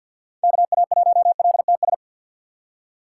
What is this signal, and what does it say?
Morse code, heard as a keyed tone, about 46 wpm, 700 Hz. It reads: KA1LTS
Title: USA Call Signs - 6x Character Spacing - 50wpm